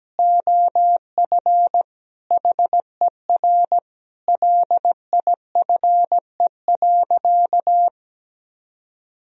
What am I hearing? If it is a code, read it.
Morse code, heard as a keyed tone, about 17 wpm, 700 Hz. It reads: OF HER LIFE.
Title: Sentences from Top 1000 Words - Review Part 3 - 17wpm